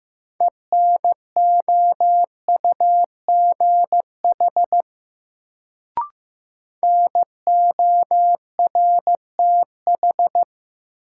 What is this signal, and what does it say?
Morse code, heard as a keyed tone, about 15 wpm, 700 Hz. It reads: ENOUGH NORTH
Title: Sentences from Top 400 Words - New Word Drill - 15wpm